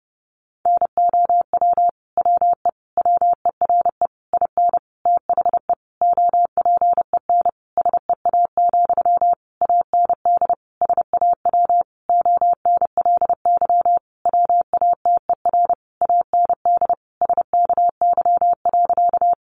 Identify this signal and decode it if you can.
Morse code, heard as a keyed tone, about 30 wpm, 700 Hz. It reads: NOW WE WERE IN THE OPEN SEA, AND SAW ONLY WATER AND SKY.
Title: Sentences from Top 1200 Words 30wpm